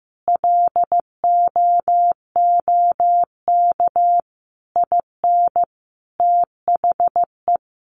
Morse code, heard as a keyed tone, about 15 wpm, 700 Hz. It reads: LOOK IN THE
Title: Sentences from Top 1300 Words 15wpm